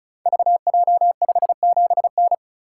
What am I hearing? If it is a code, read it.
Morse code, heard as a keyed tone, about 36 wpm, 700 Hz. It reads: VJ57N
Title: Code Groups - 5 Letter:Numbers - 35wpm